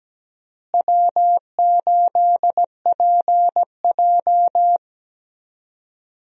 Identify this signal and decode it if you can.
Morse code, heard as a keyed tone, about 17 wpm, 700 Hz. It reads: W8PJ
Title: USA Call Signs - 4 Characters - Repeated 3x - 17wpm